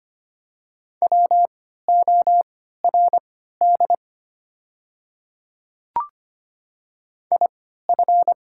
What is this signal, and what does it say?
Morse code, heard as a keyed tone, about 25 wpm, spaced as for 14 wpm, 700 Hz. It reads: WORD IF
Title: 3x Character Spacing - Top 100 Words - 25wpm